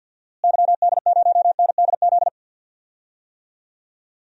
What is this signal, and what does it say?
Morse code, heard as a keyed tone, about 51 wpm, 700 Hz. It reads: YD0NDZ
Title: DXCC-100 Call Signs - 6 Characters - Repeated 3x - 50wpm